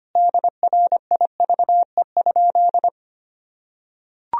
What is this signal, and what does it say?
Morse code, heard as a keyed tone, about 25 wpm, 700 Hz. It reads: DRIVE?
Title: Sentences from Top 1000 Words - Review Part 2 - 25wpm